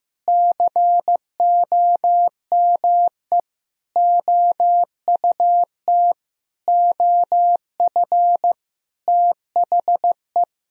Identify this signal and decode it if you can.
Morse code, heard as a keyed tone, about 16 wpm, 700 Hz. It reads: COME OUT OF THE
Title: Sentences from Top 1000 Words - Review Part 2 - 15wpm